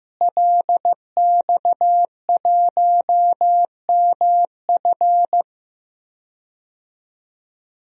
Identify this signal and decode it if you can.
Morse code, heard as a keyed tone, about 15 wpm, 700 Hz. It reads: LX1MF
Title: DXCC-100 Call Signs - 5 Characters - Repeated 2x - 15wpm